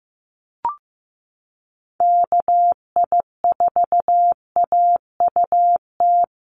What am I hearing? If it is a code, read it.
Morse code, heard as a keyed tone, about 15 wpm, 700 Hz. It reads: KI4AUT
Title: USA Call Signs Repeated 2x 15wpm